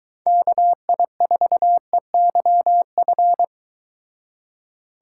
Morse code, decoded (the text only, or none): KI4EYF